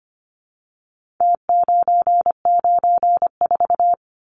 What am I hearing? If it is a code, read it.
Morse code, heard as a keyed tone, about 25 wpm, 700 Hz. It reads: T994
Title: Numbers - Contest Style - 4-Digit 25wpm